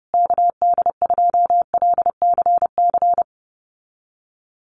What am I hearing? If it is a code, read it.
Morse code, heard as a keyed tone, about 30 wpm, 700 Hz. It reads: KD2LCC
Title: USA Call Signs Repeated 2x 30wpm